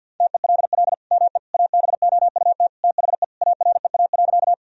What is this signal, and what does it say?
Morse code, heard as a keyed tone, about 46 wpm, 700 Hz. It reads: TELL ME ABOUT THE AREA.
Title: Sentences from Top 1000 Words - Review Part 2 - 50wpm